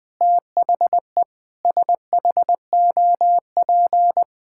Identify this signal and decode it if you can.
Morse code, heard as a keyed tone, about 20 wpm, 700 Hz. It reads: THE SHOP
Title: Sentences from Top 1600 Words 20wpm